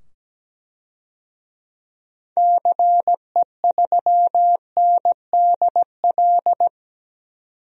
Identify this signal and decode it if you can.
Morse code, heard as a keyed tone, about 17 wpm, 700 Hz. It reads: CE3NDL
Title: DXCC-100 Call Signs - 6 Characters - 17wpm